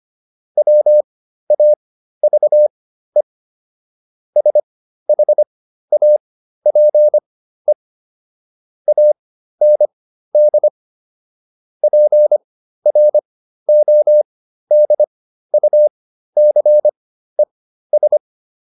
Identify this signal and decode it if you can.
Morse code, heard as a keyed tone, about 25 wpm, spaced as for 13 wpm, 600 Hz. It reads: WAVE SHAPE AND PRODUCES